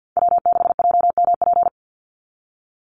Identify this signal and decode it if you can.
Morse code, heard as a keyed tone, about 48 wpm, 700 Hz. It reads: W6JMP
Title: NA SOTA Chaser Callsigns 50wpm